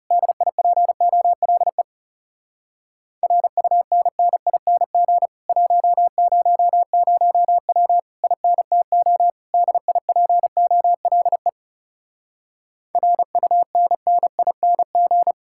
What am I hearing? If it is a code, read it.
Morse code, heard as a keyed tone, about 40 wpm, 700 Hz. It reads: DIPOLE RUNNING 100W INTO DIPOLE RUNNING